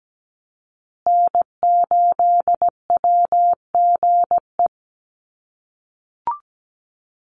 Morse code, heard as a keyed tone, about 17 wpm, 700 Hz. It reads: N8WGE
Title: USA Call Signs Repeated 3x 17wpm